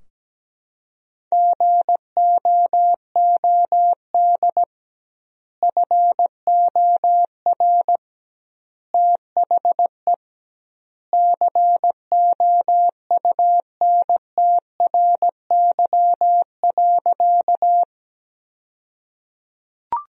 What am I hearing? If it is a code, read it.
Morse code, heard as a keyed tone, about 17 wpm, 700 Hz. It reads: GOOD FOR THE COUNTRY.
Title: 2x Word-Spacing - Sentences from Top 300 Words - 17wpm